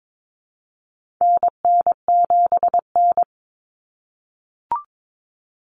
Morse code, heard as a keyed tone, about 22 wpm, 700 Hz. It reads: NN7N
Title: Common Cut Numbers - 4-Digit 22wpm